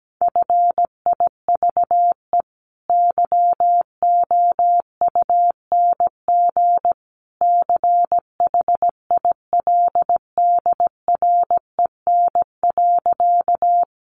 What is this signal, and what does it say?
Morse code, heard as a keyed tone, about 17 wpm, 700 Hz. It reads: FIVE YOUNG CHILDREN.
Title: Sentences from Top 400 Words 17wpm